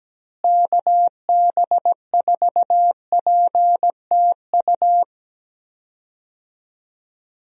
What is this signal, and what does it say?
Morse code, heard as a keyed tone, about 17 wpm, 700 Hz. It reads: KB4PTU
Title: USA Call Signs - 6 Characters - Repeated 2x - 17wpm